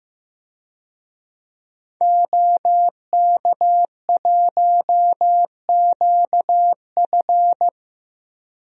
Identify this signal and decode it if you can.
Morse code, heard as a keyed tone, about 15 wpm, 700 Hz. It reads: OK1QF
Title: DXCC-100 Call Signs - 5 Characters - Repeated 2x - 15wpm